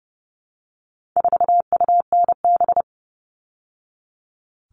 Morse code, heard as a keyed tone, about 30 wpm, 700 Hz. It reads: 4UNB
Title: All Cut Numbers - 4-Digit 30wpm